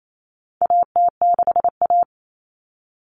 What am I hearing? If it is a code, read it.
Morse code, heard as a keyed tone, about 28 wpm, 700 Hz. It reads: AT6A
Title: Common Cut Numbers - 4-Digit 28wpm